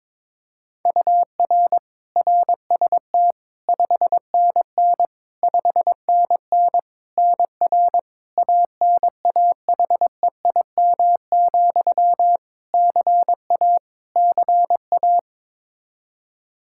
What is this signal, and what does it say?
Morse code, heard as a keyed tone, about 22 wpm, 700 Hz. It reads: UR RST 5NN 5NN NR ANAHEIM, CA CA
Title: Practice QSO - Exchange 1 - Beginning - 22wpm